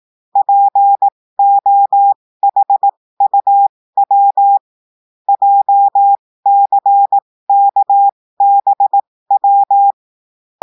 Morse code, heard as a keyed tone, about 18 wpm, spaced as for 15 wpm, 800 Hz. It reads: POHUW JCKBW
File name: cours-du-lundi files / cours-1580-texte-chiffre-15-mpm